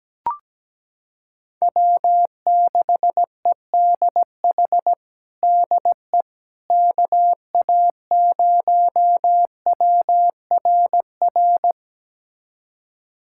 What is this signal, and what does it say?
Morse code, heard as a keyed tone, about 17 wpm, 700 Hz. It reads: W6EDH DE KA0WRR
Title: USA Callsign DE Callsign 17wpm